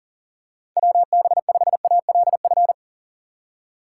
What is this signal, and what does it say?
Morse code, heard as a keyed tone, about 40 wpm, 690 Hz. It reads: WB5ALF
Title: USA Call Signs - 4x Character Spacing - 40wpm